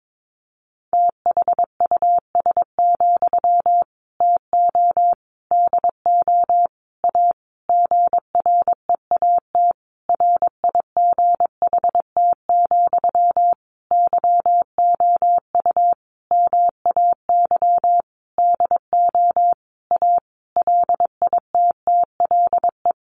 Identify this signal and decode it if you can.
Morse code, heard as a keyed tone, about 22 wpm, 700 Hz. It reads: THUS, TO DO A GREAT RIGHT, YOU MAY DO A LITTLE
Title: Sentences from Top 900 Words 22wpm